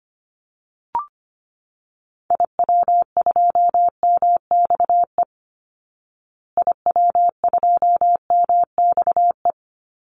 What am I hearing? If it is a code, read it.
Morse code, heard as a keyed tone, about 25 wpm, 700 Hz. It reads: IW2MXE IW2MXE
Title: Call Signs Repeated 2x - 25wpm